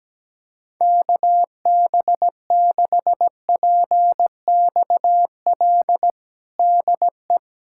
Morse code, heard as a keyed tone, about 17 wpm, 700 Hz. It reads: KB6PXL DE
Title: USA Callsign DE Callsign - 6 Characters - 17wpm